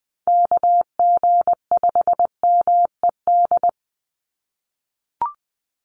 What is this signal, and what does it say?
Morse code, heard as a keyed tone, about 20 wpm, 700 Hz. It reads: KG5MED
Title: USA Call Signs 20wpm